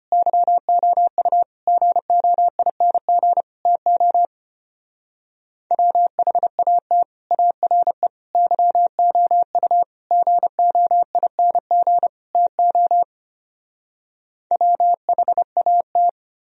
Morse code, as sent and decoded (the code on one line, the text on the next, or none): -.-- --- ..- / --. --- .. -. --. / - --- / .-- .... .- - / .- .-. / -.-- --- ..- / --. --- .. -. --. / - --- / .-- .... .- -
YOU GOING TO WHAT ARE YOU GOING TO WHAT